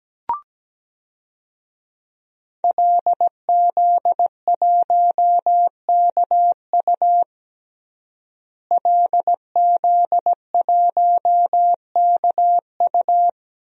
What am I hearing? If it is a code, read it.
Morse code, heard as a keyed tone, about 17 wpm, 700 Hz. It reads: LZ1KU LZ1KU
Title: DXCC-100 Call Signs - 5 Characters - Repeated 3x - 17wpm